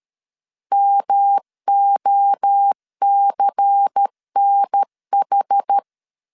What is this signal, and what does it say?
Morse code, heard as a keyed tone, about 13 wpm, 800 Hz. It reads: MOCNH
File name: CW / DET / 1 6 1500C S B 12wpm 3431